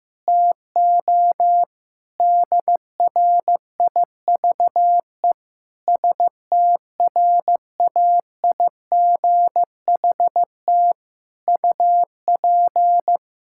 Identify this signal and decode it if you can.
Morse code, heard as a keyed tone, about 15 wpm, 700 Hz. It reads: TO DRIVE STRAIGHT UP
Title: Sentences from Top 1000 Words - Review Part 6 - 15wpm